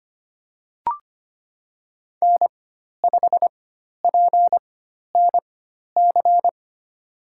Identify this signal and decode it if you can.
Morse code, heard as a keyed tone, about 25 wpm, spaced as for 12 wpm, 700 Hz. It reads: N5PNC
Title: USA Call Signs - 4x Character Spacing - 25wpm